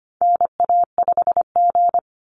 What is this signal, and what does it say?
Morse code, heard as a keyed tone, about 25 wpm, 700 Hz. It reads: NA5G
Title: Call Signs Repeated 3x - 25wpm